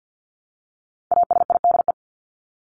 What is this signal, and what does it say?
Morse code, heard as a keyed tone, about 50 wpm, 700 Hz. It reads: ASIDE
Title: Top 1751-2000 Words 50wpm